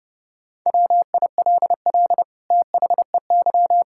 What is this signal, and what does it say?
Morse code, heard as a keyed tone, about 32 wpm, 700 Hz. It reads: WILL THEY
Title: Sentences from Top 1000 Words - Review Part 6 - 30wpm